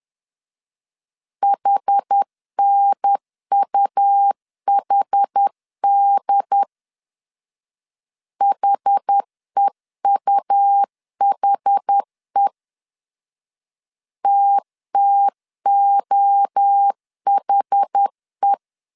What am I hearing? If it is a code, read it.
Morse code, heard as a keyed tone, about 10 wpm, 800 Hz. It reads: HNUHD HEUHE TTOHE
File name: 10wpm